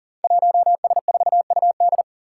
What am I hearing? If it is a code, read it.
Morse code, heard as a keyed tone, about 40 wpm, 700 Hz. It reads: 1S4UD
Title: Five Challenging Letter-Number Combinations 40wpm